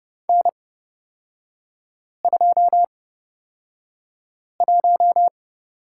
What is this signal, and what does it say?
Morse code, heard as a keyed tone, about 30 wpm, spaced as for 5 wpm, 700 Hz. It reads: N21
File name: Lesson 36e - BT - Three Characters 30wpm